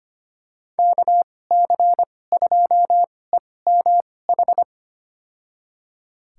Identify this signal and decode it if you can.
Morse code, heard as a keyed tone, about 25 wpm, spaced as for 18 wpm, 700 Hz. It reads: KC2EMH
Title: USA Call Signs - 6 Characters - 2x Character Spacing - 25wpm